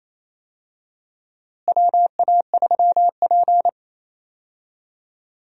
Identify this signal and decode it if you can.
Morse code, heard as a keyed tone, about 28 wpm, 700 Hz. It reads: WA3P
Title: USA Call Signs - 4 Characters - Repeated 3x - 28wpm